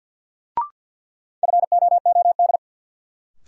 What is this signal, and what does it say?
Morse code, heard as a keyed tone, about 53 wpm, 700 Hz. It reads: FOOD
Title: Sets of 5 Words - New Word Drill - 50wpm